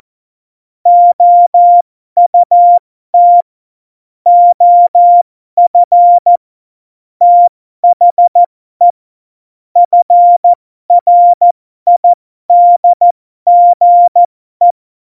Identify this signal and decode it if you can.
Morse code, heard as a keyed tone, about 14 wpm, spaced as for 12 wpm, 700 Hz. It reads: OUT OF THE FRIDGE
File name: Morsekurs / Witze